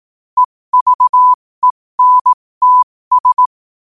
Morse code, heard as a keyed tone, about 18 wpm, spaced as for 15 wpm, 1000 Hz. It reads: EVENTS
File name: qotdfiles / 15wpm